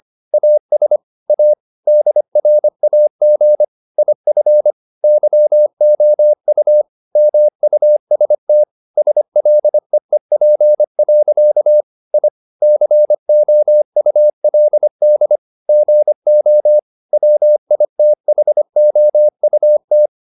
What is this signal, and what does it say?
Morse code, heard as a keyed tone, about 25 wpm, 600 Hz. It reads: AS A DRAG IF YOU MUST SLEEP. I COULD GO WITHOUT